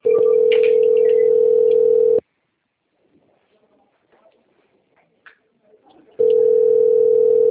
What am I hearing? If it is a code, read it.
Ringback tone (North America), 440 + 480 Hz.
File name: out- - - - - .63365